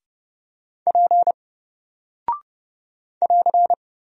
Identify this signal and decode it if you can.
Morse code, heard as a keyed tone, about 30 wpm, 700 Hz. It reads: P <AR>
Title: Single Character (Letter:Number:Punctuation:Prosign) 30wpm